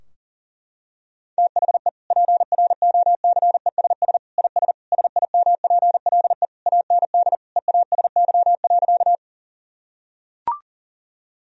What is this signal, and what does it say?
Morse code, heard as a keyed tone, about 40 wpm, 700 Hz. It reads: THE PROCESS IS SIMPLE AND EASY.
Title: Sentences from Top 1000 Words - Review Part 3 - 40wpm